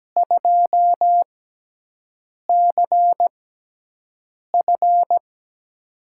Morse code, heard as a keyed tone, about 17 wpm, spaced as for 6 wpm, 700 Hz. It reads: 2CF
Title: USA Call Signs - 5 Characters - 6x Character Spacing - 17wpm